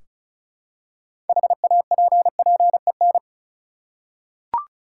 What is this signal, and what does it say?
Morse code, heard as a keyed tone, about 36 wpm, 700 Hz. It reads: HAPPEN
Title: Six Letter Words - 35wpm